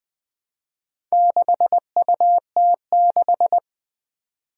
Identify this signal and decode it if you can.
Morse code, heard as a keyed tone, about 20 wpm, 700 Hz. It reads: 6UT6
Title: Four Challenging Letter-Number Combinations 20wpm